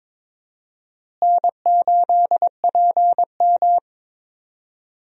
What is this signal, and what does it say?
Morse code, heard as a keyed tone, about 21 wpm, 700 Hz. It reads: N8PM
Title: USA Call Signs - 4 Characters - Repeated 2x - 22wpm